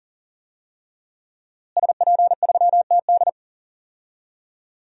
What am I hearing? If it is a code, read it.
Morse code, heard as a keyed tone, about 40 wpm, 700 Hz. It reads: SP3TD
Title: DXCC-100 Call Signs - 5 Characters - Repeated 3x - 40wpm